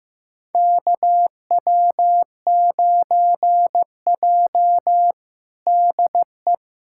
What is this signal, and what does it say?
Morse code, heard as a keyed tone, about 16 wpm, 700 Hz. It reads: KW9J DE
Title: USA Callsign DE Callsign - 4 Characters - 15wpm